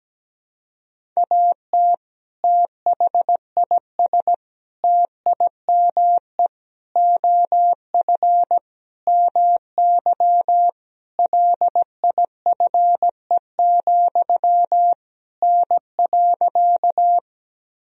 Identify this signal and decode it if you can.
Morse code, heard as a keyed tone, about 17 wpm, 700 Hz. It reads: AT THIS TIME OF MY LIFE, N.
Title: Sentences from Top 300 Words and Abbreviations 17wpm